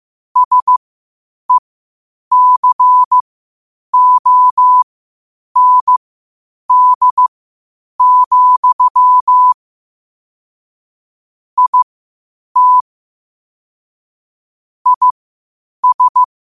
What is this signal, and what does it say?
Morse code, heard as a keyed tone, about 15 wpm, spaced as for 8 wpm, 1000 Hz. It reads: SECOND, IT IS
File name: qotdfiles / 8wpm